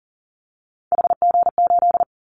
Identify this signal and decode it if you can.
Morse code, heard as a keyed tone, about 40 wpm, 700 Hz. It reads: HG8
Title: Callsign Prefixes for Top 25 DXCC - Repeated 3x - 40wpm